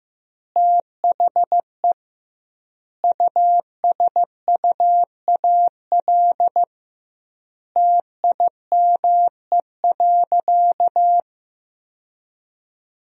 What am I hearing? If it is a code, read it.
Morse code, heard as a keyed tone, about 15 wpm, 700 Hz. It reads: THE USUAL TIME.
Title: 2x Word-Spacing - Sentences from Top 400 Words - 15wpm